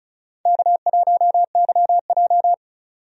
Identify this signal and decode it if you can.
Morse code, heard as a keyed tone, about 34 wpm, 700 Hz. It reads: K1YJ
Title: USA Call Signs - 4 Characters - 35wpm